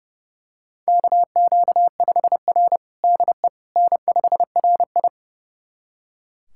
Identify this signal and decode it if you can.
Morse code, heard as a keyed tone, about 32 wpm, 700 Hz. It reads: KQ5R DE N5RI
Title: USA Callsign DE Callsign - 4 Characters - 30wpm